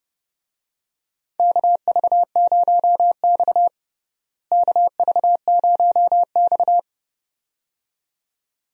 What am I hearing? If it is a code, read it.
Morse code, heard as a keyed tone, about 32 wpm, 700 Hz. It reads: KV0X KV0X
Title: USA Call Signs - 4 Characters - Repeated 2x - 30wpm